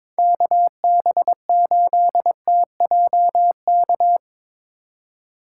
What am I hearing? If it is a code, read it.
Morse code, heard as a keyed tone, about 22 wpm, 700 Hz. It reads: KB8TJK
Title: USA Call Signs - 2x Character Spacing - 22wpm